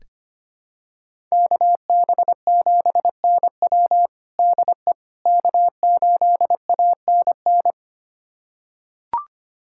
Morse code, heard as a keyed tone, about 25 wpm, 700 Hz. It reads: KB7NW DE K8ANN
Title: USA Callsign DE Callsign - 5 Characters - 25wpm